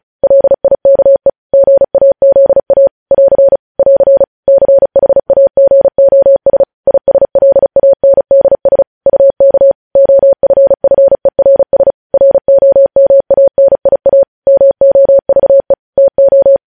Morse code, heard as a keyed tone, about 36 wpm, 550 Hz. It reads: LIKE GAZA <AR> <AR> CHAGOS ISLANDS UK OFFERS ROMANIA MOVE TO